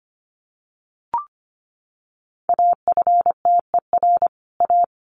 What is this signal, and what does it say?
Morse code, heard as a keyed tone, about 25 wpm, 700 Hz. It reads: AFTER A
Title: Sets of 2 Words 25wpm